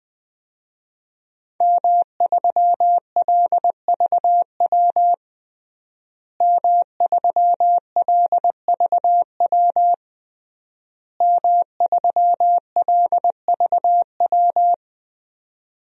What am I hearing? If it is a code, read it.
Morse code, heard as a keyed tone, about 21 wpm, 700 Hz. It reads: M3LVW M3LVW M3LVW
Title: DXCC-100 Call Signs - 5 Characters - Repeated 3x - 20wpm